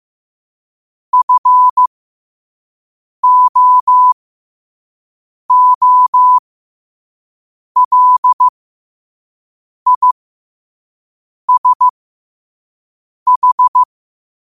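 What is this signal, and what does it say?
Morse code, heard as a keyed tone, about 15 wpm, spaced as for 5 wpm, 1000 Hz. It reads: FOOLISH
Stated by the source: Morse code quote of the day at 5 WPM.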